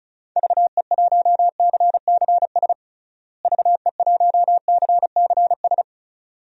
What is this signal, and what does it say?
Morse code, heard as a keyed tone, about 36 wpm, 700 Hz. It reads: VE1CCS VE1CCS
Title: DXCC-100 Call Signs - 6 Characters - Repeated 2x - 35wpm